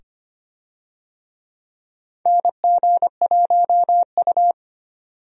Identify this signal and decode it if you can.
Morse code, heard as a keyed tone, about 25 wpm, 700 Hz. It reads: NG1U
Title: USA Call Signs - 4 Characters - 25wpm